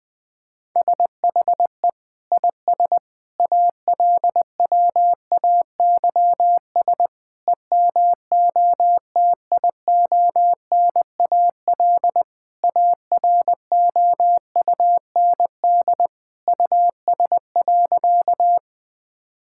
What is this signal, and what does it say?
Morse code, heard as a keyed tone, about 20 wpm, 700 Hz. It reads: SHE IS ALWAYS EMOTIONAL AROUND US.
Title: Sentences from Top 1000 Words - Review Part 5 - 20wpm